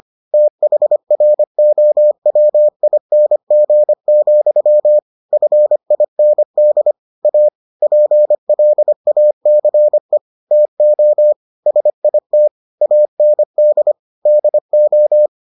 Morse code, heard as a keyed tone, about 25 wpm, 600 Hz. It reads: THROWING, FIND A PLACE TO SIT AND DO